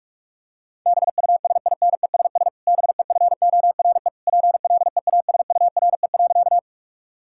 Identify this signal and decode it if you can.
Morse code, heard as a keyed tone, about 46 wpm, 700 Hz. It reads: BUSINESS BEFORE PLEASURE.
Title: Sentences from Top 1600 Words 45wpm